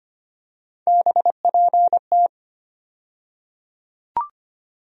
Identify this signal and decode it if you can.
Morse code, heard as a keyed tone, about 25 wpm, 700 Hz. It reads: BPT
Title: USA Call Sign Postfixes - 3 Letters 25wpm